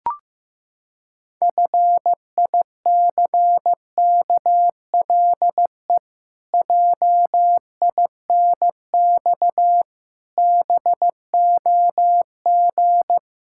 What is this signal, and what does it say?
Morse code, heard as a keyed tone, about 15 wpm, 700 Hz. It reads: FICKLE JINX BOG
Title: Pangram Challenge v2 - 15wpm